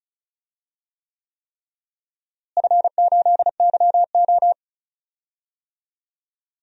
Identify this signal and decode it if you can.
Morse code, heard as a keyed tone, about 36 wpm, 700 Hz. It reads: F8YO
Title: DXCC-100 Call Signs - 4 Characters - Repeated 3x - 35wpm